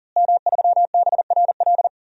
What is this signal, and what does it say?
Morse code, heard as a keyed tone, about 40 wpm, 700 Hz. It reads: M3BRL
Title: DXCC-100 Call Signs - 5 Characters - Repeated 3x - 40wpm